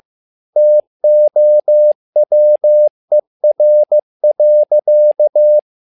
Morse code, heard as a keyed tone, about 15 wpm, 600 Hz. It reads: TOWER.
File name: learning center / tower trouble / tower trouble 15wpm